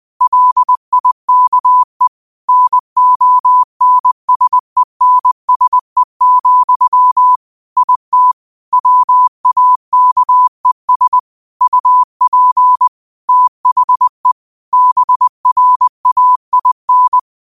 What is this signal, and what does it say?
Morse code, heard as a keyed tone, about 20 wpm, 1000 Hz. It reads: LIKE NONSENSE, IT WAKES UP THE BRAIN
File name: qotdfiles / 20wpm